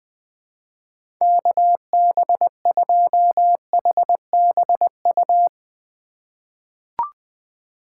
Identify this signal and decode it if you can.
Morse code, heard as a keyed tone, about 21 wpm, 700 Hz. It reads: KB2HBU
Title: USA Call Signs - 6 Characters - 3x Character Spacing - 20wpm